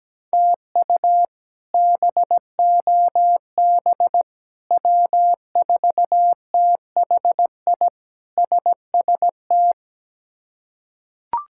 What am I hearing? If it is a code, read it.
Morse code, heard as a keyed tone, about 17 wpm, 700 Hz. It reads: TU BOB W4THI SST
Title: K1USN SST QSO Practice - 17wpm